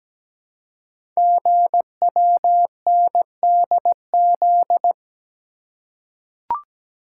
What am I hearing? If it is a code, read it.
Morse code, heard as a keyed tone, about 17 wpm, 700 Hz. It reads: GWNDZ
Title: Code Groups - 5 Letters - 17wpm